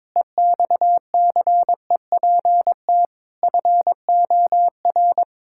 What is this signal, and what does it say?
Morse code, heard as a keyed tone, about 22 wpm, 700 Hz. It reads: EXCEPT FOR
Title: Pangram Challenge v2 - 22wpm